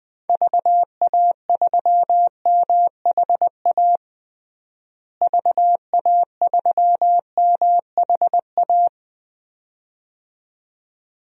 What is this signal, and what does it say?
Morse code, heard as a keyed tone, about 20 wpm, 700 Hz. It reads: VA3MHA VA3MHA
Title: DXCC-100 Call Signs - 6 Characters - Repeated 2x - 20wpm